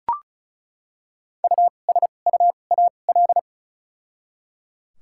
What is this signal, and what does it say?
Morse code, heard as a keyed tone, about 34 wpm, spaced as for 25 wpm, 700 Hz. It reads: USUAL
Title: 2x Character Spacing - Top 500 Words - 35wpm